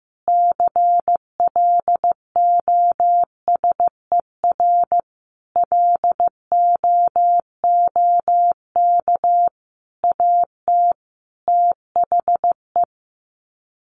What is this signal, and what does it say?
Morse code, heard as a keyed tone, about 15 wpm, 700 Hz. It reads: CLOSER LOOK AT THE
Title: Sets of 5 Words Encore 15wpm